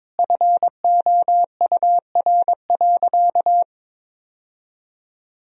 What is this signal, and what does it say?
Morse code, heard as a keyed tone, about 22 wpm, 700 Hz. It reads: FOUR.
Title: Sentences from Top 1000 Words - Review Part 1 - 22wpm